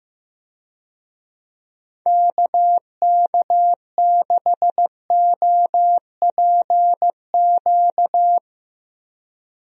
Morse code, heard as a keyed tone, about 15 wpm, 700 Hz. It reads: KK6OPQ
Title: USA Call Signs - 6 Characters - 15wpm